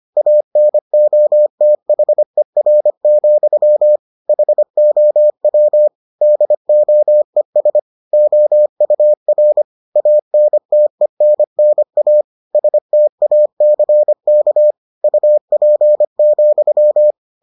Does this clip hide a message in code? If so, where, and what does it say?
Yes: Morse code from 0.2 to 17.1 s, heard as a keyed tone, about 25 wpm, 600 Hz. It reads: ANOTHER, HOW DOES OUR ANTENNA STACK UP,